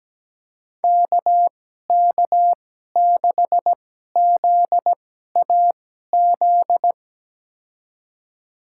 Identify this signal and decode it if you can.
Morse code, heard as a keyed tone, about 17 wpm, spaced as for 12 wpm, 700 Hz. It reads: KK6ZAZ
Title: USA Call Signs - 6 Characters - 2x Character Spacing - 17wpm